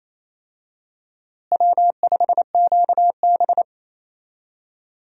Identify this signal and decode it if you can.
Morse code, heard as a keyed tone, about 28 wpm, 700 Hz. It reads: W5QB
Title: USA Call Signs - 4 Characters - Repeated 3x - 28wpm